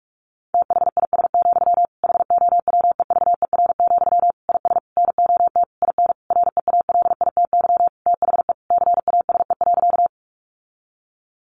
Morse code, heard as a keyed tone, about 44 wpm, 700 Hz. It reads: THIS, HOWEVER, IS NOT IN REALITY THE CASE.
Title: Sentences from Top 1200 Words 45wpm